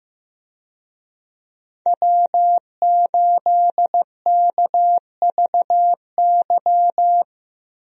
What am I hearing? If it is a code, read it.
Morse code, heard as a keyed tone, about 15 wpm, 700 Hz. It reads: W8KVY
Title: USA Call Signs - 5 Characters - Repeated 2x - 15wpm